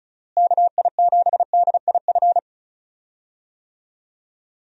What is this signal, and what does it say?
Morse code, heard as a keyed tone, about 36 wpm, 700 Hz. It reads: KI7DIF
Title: USA Call Signs - 6 Characters - Repeated 3x - 35wpm